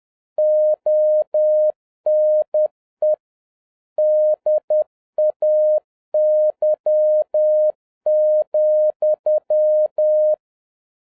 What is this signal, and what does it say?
Morse code, heard as a keyed tone, about 10 wpm, 610 Hz. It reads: ONE DAY,
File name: 2011-10-mp3 / n11-MEETING WITH GOOD FRIENDS 10wpm